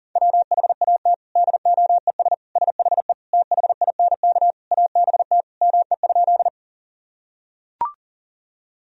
Morse code, heard as a keyed tone, about 37 wpm, 700 Hz. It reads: WHAT DOES SHE THINK ABT ME?
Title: Sentences from Top 200 Words and Abbreviations 40wpm